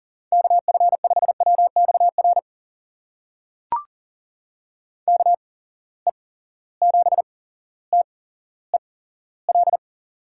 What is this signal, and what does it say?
Morse code, heard as a keyed tone, about 42 wpm, spaced as for 17 wpm, 700 Hz. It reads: KF5WXR KE7TEL
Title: USA Call Signs - 6 Characters - 8x Character Spacing - 40wpm